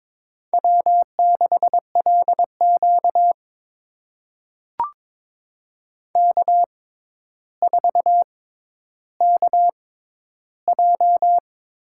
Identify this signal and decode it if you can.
Morse code, heard as a keyed tone, about 22 wpm, spaced as for 11 wpm, 700 Hz. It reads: W6LQ K4KJ
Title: USA Call Signs - 6x Character Spacing - 22wpm